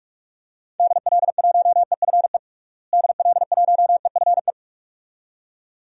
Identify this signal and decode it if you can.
Morse code, heard as a keyed tone, about 48 wpm, 700 Hz. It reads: DL1EFE DL1EFE
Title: DXCC-100 Call Signs - 6 Characters - Repeated 2x - 45wpm